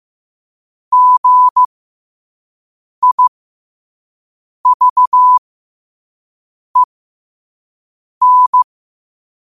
Morse code, heard as a keyed tone, about 15 wpm, spaced as for 5 wpm, 1000 Hz. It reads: GIVEN
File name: qotdfiles / 5wpm